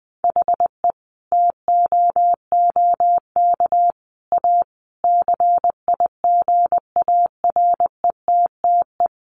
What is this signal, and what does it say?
Morse code, heard as a keyed tone, about 20 wpm, 700 Hz. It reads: HE TOOK A CIGARETTE
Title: Sentences from Top 1000 Words 20wpm